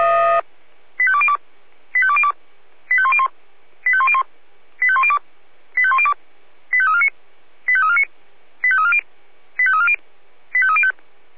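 Five-tone selective call (ZVEI), heard as quick runs of five tones, tones of about 70 ms. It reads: 85202 85202 85101 85101 85102 85102 85349 85349 85340 85340 85305
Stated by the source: ZVEI five-tone call received on SDR.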